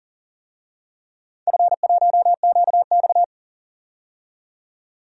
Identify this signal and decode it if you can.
Morse code, heard as a keyed tone, about 40 wpm, 700 Hz. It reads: F1QX
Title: DXCC-100 Call Signs - 4 Characters - 40wpm